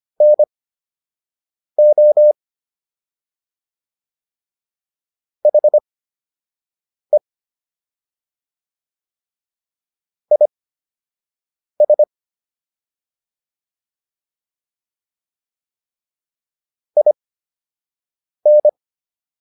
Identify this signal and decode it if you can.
Morse code, heard as a keyed tone, about 25 wpm, spaced as for 6 wpm, 600 Hz. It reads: NO HE IS IN